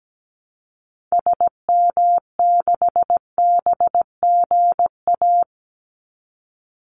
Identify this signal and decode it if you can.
Morse code, heard as a keyed tone, about 17 wpm, 700 Hz. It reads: SM6BGA
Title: Call Signs Repeated 2x - 17wpm